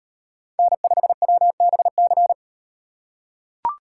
Morse code, heard as a keyed tone, about 36 wpm, 700 Hz. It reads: N5WBC